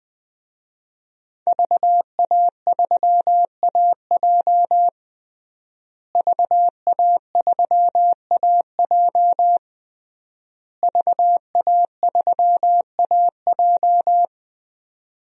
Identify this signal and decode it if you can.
Morse code, heard as a keyed tone, about 20 wpm, 700 Hz. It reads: VA3AJ VA3AJ VA3AJ
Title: DXCC-100 Call Signs - 5 Characters - Repeated 3x - 20wpm